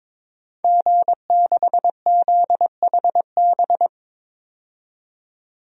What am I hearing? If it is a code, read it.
Morse code, heard as a keyed tone, about 22 wpm, 700 Hz. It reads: G6ZHB
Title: DXCC-100 Call Signs - 5 Characters - 22wpm